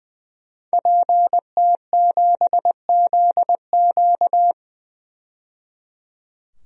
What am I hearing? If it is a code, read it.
Morse code, heard as a keyed tone, about 20 wpm, 700 Hz. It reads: PT7ZQ
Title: DXCC-100 Call Signs - 5 Characters - 20wpm